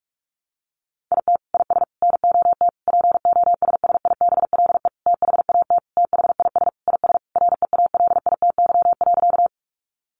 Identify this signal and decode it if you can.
Morse code, heard as a keyed tone, about 44 wpm, 700 Hz. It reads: IT IS NOT POSSIBLE THAT THIS IS REALITY.
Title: Sentences from Top 1200 Words 45wpm